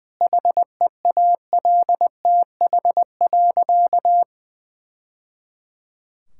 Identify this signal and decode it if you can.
Morse code, heard as a keyed tone, about 20 wpm, 700 Hz. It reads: HEALTH.
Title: Sentences from Top 1000 Words - Review Part 5 - 20wpm